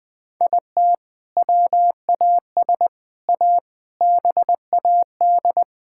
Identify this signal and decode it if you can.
Morse code, heard as a keyed tone, about 20 wpm, 700 Hz. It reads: IT WAS A BAD
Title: Sentences from Top 1000 Words - Review Part 6 - 20wpm